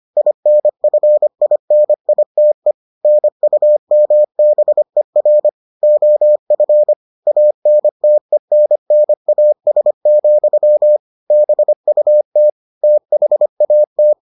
Morse code, heard as a keyed tone, about 25 wpm, 600 Hz. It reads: INFINITE NUMBER OF ANTENNAS, BUT THAT